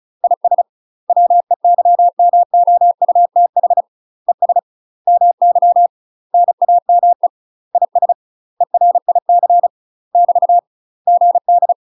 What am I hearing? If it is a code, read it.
Morse code, heard as a keyed tone, about 34 wpm, 700 Hz. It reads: IS WEYMOUTH ES MY NAME IS ERIC <BT> GD
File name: UK MASTER QSOs TRAINING / 35WPM / UK MASTER QSOs 0028